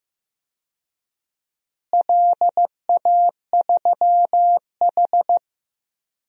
Morse code, heard as a keyed tone, about 15 wpm, 700 Hz. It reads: LA3H
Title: DXCC-100 Call Signs - 4 Characters - Repeated 3x - 15wpm